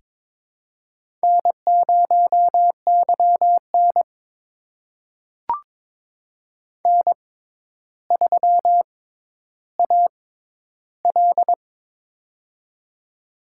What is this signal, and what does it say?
Morse code, heard as a keyed tone, about 22 wpm, spaced as for 11 wpm, 700 Hz. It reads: N0YN N3AL
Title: USA Call Signs - 4 Characters - 6x Character Spacing - 22wpm